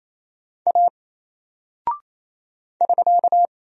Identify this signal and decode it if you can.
Morse code, heard as a keyed tone, about 27 wpm, 700 Hz. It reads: A <SK>
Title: Single Character (Letter:Number:Punctuation:Prosign) 28wpm